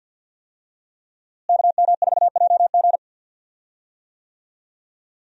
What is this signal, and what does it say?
Morse code, heard as a keyed tone, about 51 wpm, 700 Hz. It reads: KM4JG
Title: USA Call Signs - 5 Characters - Repeated 2x - 50wpm